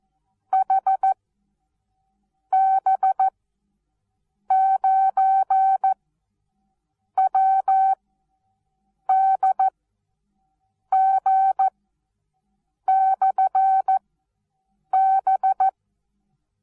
Morse code, heard as a keyed tone, about 14 wpm, spaced as for 6 wpm, 770 Hz. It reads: HB9WDG/B
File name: Onlinelehrgang / morsen / DJ4UF Morsen19